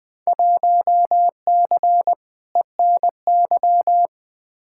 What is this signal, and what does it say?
Morse code, heard as a keyed tone, about 20 wpm, 700 Hz. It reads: 1C ENY
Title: Field Day Exchanges Encore - 20wpm